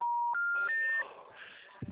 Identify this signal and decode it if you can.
Special information tone, with noise.